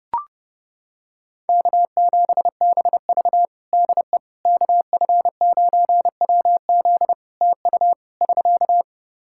Morse code, heard as a keyed tone, about 30 wpm, 700 Hz. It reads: K7BV DE KF9WZ TU <SK>
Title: K1USN SST QSO Practice 30wpm